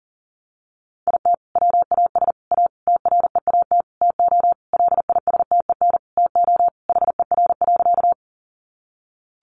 Morse code, heard as a keyed tone, about 40 wpm, 700 Hz. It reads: IT WAS A TREAT TO LISTEN TO HER.
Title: Sentences from Top 1300 Words 40wpm